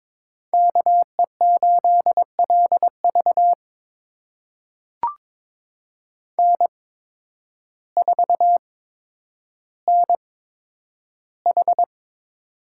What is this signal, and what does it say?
Morse code, heard as a keyed tone, about 22 wpm, spaced as for 10 wpm, 700 Hz. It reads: KE8LV N4NH
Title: USA Call Signs - 5 Characters - 8x Character Spacing - 22wpm